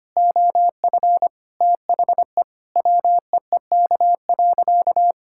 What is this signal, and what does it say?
Morse code, heard as a keyed tone, about 25 wpm, 700 Hz. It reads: OF THE WEEK.
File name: practice / 25wpm / Sentences with 4-Word Phrases - 25wpm